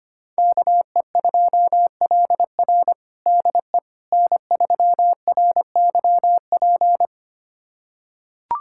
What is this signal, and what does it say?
Morse code, heard as a keyed tone, about 25 wpm, 700 Hz. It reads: KE2LR DE N3RYP
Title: USA Callsign DE Callsign - 5 Characters - 25wpm